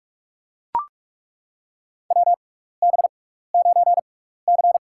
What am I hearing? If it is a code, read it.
Morse code, heard as a keyed tone, about 42 wpm, spaced as for 15 wpm, 700 Hz. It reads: WB9C
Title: USA Call Signs - 6x Character Spacing - 45wpm